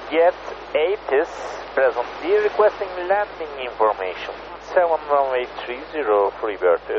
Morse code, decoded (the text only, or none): none